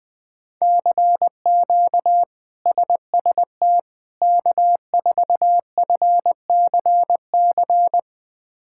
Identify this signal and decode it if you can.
Morse code, heard as a keyed tone, about 20 wpm, 700 Hz. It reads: CQ SST K4FCC
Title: K1USN SST QSO Practice - 20wpm